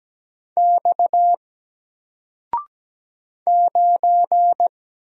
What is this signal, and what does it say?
Morse code, heard as a keyed tone, about 17 wpm, 700 Hz. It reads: X 9
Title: Single Letter-Number Alt 17wpm